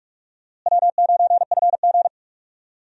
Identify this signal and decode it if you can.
Morse code, heard as a keyed tone, about 42 wpm, 700 Hz. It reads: W9FG
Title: USA Call Signs - 4 Characters - Repeated 2x - 45wpm